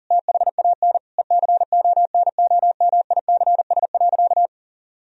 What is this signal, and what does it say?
Morse code, heard as a keyed tone, about 40 wpm, 700 Hz. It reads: THAN ECONOMICS.
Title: Sentences from Top 1000 Words - Review Part 3 - 40wpm